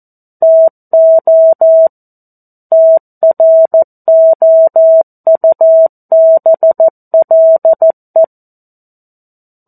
Morse code, heard as a keyed tone, about 14 wpm, 650 Hz. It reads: TO TROUBLE
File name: 02-Chapter-02 14 WPM